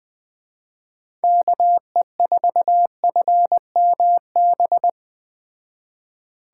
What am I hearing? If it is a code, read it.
Morse code, heard as a keyed tone, about 21 wpm, 700 Hz. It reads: KE4FMB